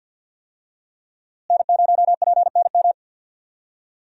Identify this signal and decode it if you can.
Morse code, heard as a keyed tone, about 50 wpm, 700 Hz. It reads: N0PNM
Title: USA Call Signs - 5 Characters - Repeated 2x - 50wpm